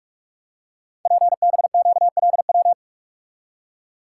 Morse code, heard as a keyed tone, about 44 wpm, 700 Hz. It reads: PBQLW
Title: Code Groups - 5 Letters - 45wpm